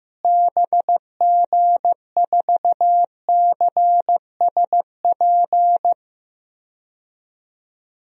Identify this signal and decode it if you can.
Morse code, heard as a keyed tone, about 16 wpm, 700 Hz. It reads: BG4CSP
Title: DXCC-100 Call Signs - 6 Characters - Repeated 3x - 15wpm